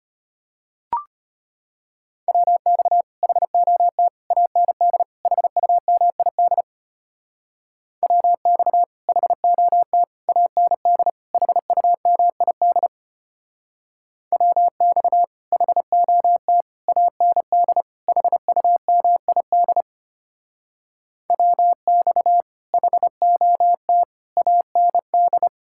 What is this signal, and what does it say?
Morse code, heard as a keyed tone, about 33 wpm, 700 Hz. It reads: WX HOT AND HUMID WX HOT AND HUMID WX HOT AND HUMID WX HOT AND